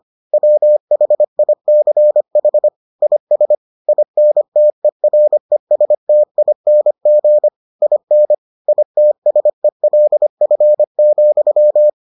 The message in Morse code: WHICH IS INTERESTING IN ITSELF,

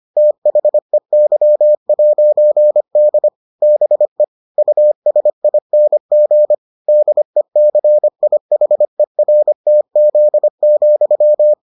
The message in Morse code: THEY'D BE USING DECIHERTZ,

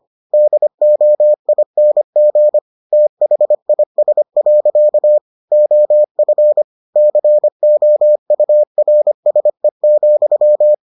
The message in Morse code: DOING THIS. OF COURSE,

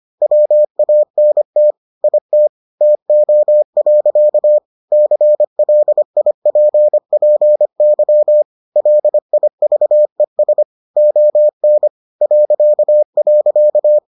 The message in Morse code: WANT IT TO. CLIPPY LIVES ON ..

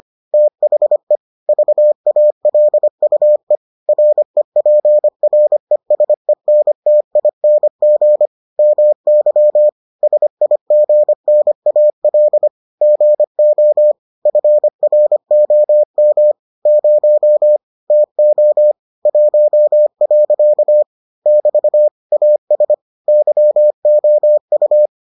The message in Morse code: THE VALUE REPRESENTING MY SIGNAL GO FROM 0 TO 1. <BT> AS YOU

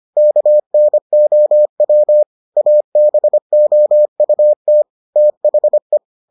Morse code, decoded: KNOW ABOUT THE

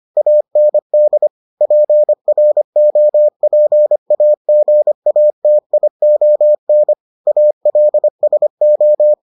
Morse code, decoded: AND PROPAGATION ALSO

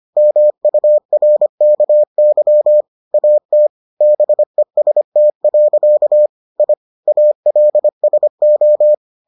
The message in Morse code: MURKY AT BEST. I ALSO

